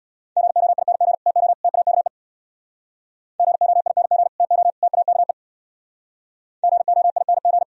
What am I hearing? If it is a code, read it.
Morse code, heard as a keyed tone, about 65 wpm, 700 Hz. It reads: GOING TO TAKE GOING TO TAKE GOING